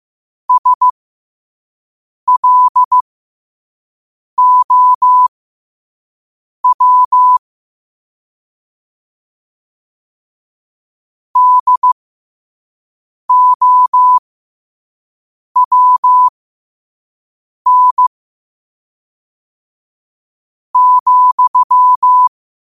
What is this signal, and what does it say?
Morse code, heard as a keyed tone, about 15 wpm, spaced as for 5 wpm, 1000 Hz. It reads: SLOW DOWN ,
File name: qotdfiles / 5wpm